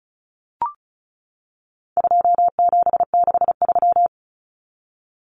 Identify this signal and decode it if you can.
Morse code, heard as a keyed tone, about 34 wpm, 700 Hz. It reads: 2763
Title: Common Cut Numbers - 4-Digit 35wpm